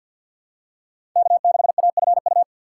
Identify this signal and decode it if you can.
Morse code, heard as a keyed tone, about 51 wpm, 700 Hz. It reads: K6AFU
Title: USA Call Signs - 5 Characters - 4x Character Spacing - 50wpm